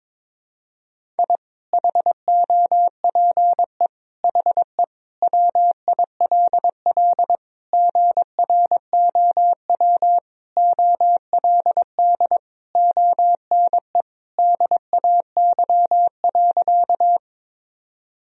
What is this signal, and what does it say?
Morse code, heard as a keyed tone, about 21 wpm, 700 Hz. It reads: I HOPE HE WILL GROW OLD ONE DAY.
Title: Sentences from Top 1000 Words - Review Part 1 - 22wpm